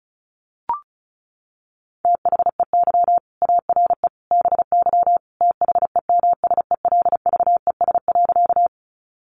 Morse code, heard as a keyed tone, about 34 wpm, 700 Hz. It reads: THEY ARE BY THEMSELVES.
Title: Sentences from Top 800 Words 35wpm